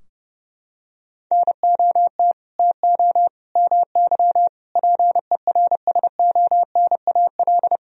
Morse code, heard as a keyed tone, about 30 wpm, 700 Hz. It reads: NOT TO MY PERSONAL
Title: Sentences from Top 1000 Words - Review Part 4 - 30wpm